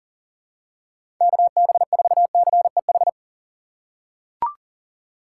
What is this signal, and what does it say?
Morse code, heard as a keyed tone, about 40 wpm, 690 Hz. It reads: KB4CEH
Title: USA Call Signs - 6 Characters - 2x Character Spacing - 40wpm